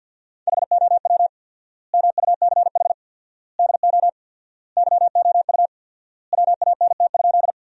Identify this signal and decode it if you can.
Morse code, heard as a keyed tone, about 50 wpm, 700 Hz. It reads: HOW MUCH DO YOU WANT?
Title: 4x Word-Spacing - Sentences from Top 200 Words - 50wpm